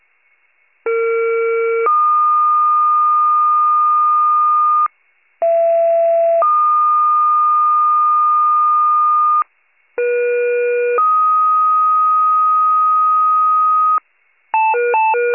Two-tone page, heard as a one-second tone, then a three-second tone, about 460 then 1220 Hz, about 670 then 1220 Hz, about 480 then 1250 Hz. It is heard from 0.9 to 14.0 s.